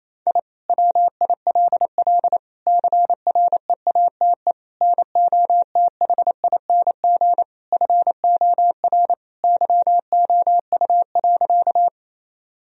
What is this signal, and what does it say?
Morse code, heard as a keyed tone, about 28 wpm, 700 Hz. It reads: I WILL CREATE NOTHING FOR YOU.
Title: Sentences from Top 1000 Words - Review Part 3 - 28wpm